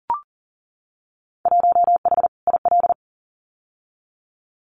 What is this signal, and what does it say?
Morse code, heard as a keyed tone, about 38 wpm, 700 Hz. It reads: 1H IL
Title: Winter Field Day Exchanges 40wpm